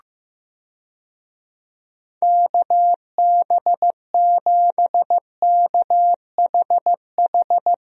Morse code, heard as a keyed tone, about 15 wpm, 700 Hz. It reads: KB7KHH
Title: USA Call Signs - 6 Characters - Repeated 2x - 15wpm